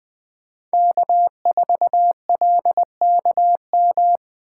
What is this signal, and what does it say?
Morse code, heard as a keyed tone, about 20 wpm, 700 Hz. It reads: K4LKM
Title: Speed Racing - USA 1x3 Call Signs 17wpm